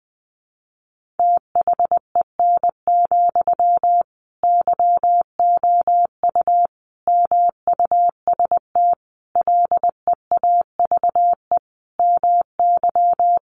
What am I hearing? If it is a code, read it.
Morse code, heard as a keyed tone, about 19 wpm, 700 Hz. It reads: THEN, YOU MUST LEAVE MY
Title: Sentences from Top 1500 Words 20wpm